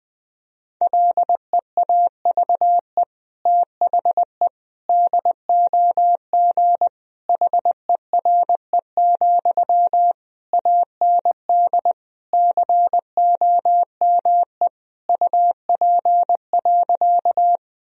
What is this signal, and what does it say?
Morse code, heard as a keyed tone, about 21 wpm, 700 Hz. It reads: LEAVE THE DOG HERE, AND COME UP.
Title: Sentences from Top 1000 Words - Review Part 2 - 20wpm